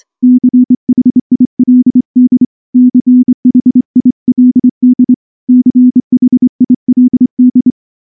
Morse code, heard as a keyed tone, about 24 wpm, 250 Hz. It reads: CHILD CHILD CHILD
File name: morse / morse-training-25WPM